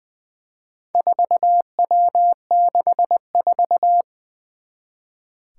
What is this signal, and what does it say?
Morse code, heard as a keyed tone, about 20 wpm, 700 Hz. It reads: 4W64
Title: Four Challenging Letter-Number Combinations 20wpm